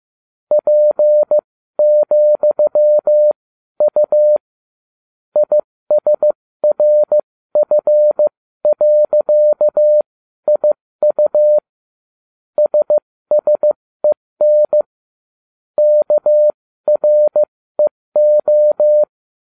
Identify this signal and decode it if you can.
Morse code, heard as a keyed tone, about 15 wpm, spaced as for 13 wpm, 600 Hz. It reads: P,U ISRF.IU SSEN KREO